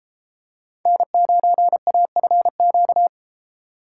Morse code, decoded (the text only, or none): N9AFQ